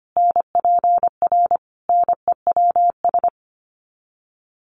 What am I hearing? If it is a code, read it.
Morse code, heard as a keyed tone, about 25 wpm, 700 Hz. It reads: NPR NEWS
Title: Sets of 5 Words 25wpm